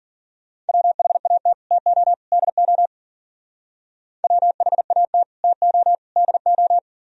Morse code, heard as a keyed tone, about 48 wpm, 700 Hz. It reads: WHAT TO DO WHAT TO DO
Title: Speed-Racing - Sets of 3 Words 40wpm